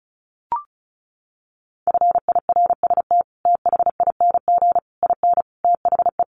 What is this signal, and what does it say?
Morse code, heard as a keyed tone, about 34 wpm, 700 Hz. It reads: FIRST THING IN THE
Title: Sets of 5 Words Encore2 35wpm